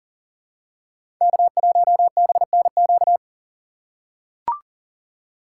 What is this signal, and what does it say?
Morse code, heard as a keyed tone, about 40 wpm, 700 Hz. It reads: K1BNQ
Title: USA Call Signs - 3x Character Spacing - 40wpm